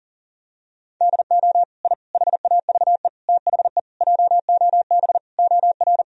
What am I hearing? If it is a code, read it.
Morse code, heard as a keyed tone, about 37 wpm, 700 Hz. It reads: DO I HAVE THE JOB OR
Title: Sentences from Top 1000 Words - Review Part 3 - 40wpm